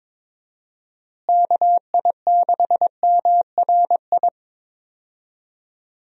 Morse code, heard as a keyed tone, about 22 wpm, 700 Hz. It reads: KI6MRI
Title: USA Call Signs - 6 Characters - Repeated 3x - 22wpm